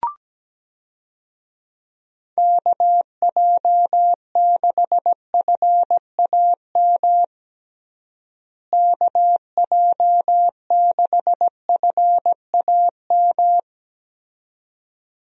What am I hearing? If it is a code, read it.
Morse code, heard as a keyed tone, about 17 wpm, 700 Hz. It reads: KJ6FAM KJ6FAM
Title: USA Call Signs - 6 Characters - Repeated 2x - 17wpm